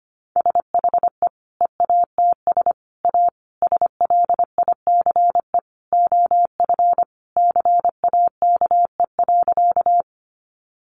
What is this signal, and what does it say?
Morse code, heard as a keyed tone, about 25 wpm, 700 Hz. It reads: SHE EATS A SLICE OF CAKE.
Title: Additional Easy Phrases 25wpm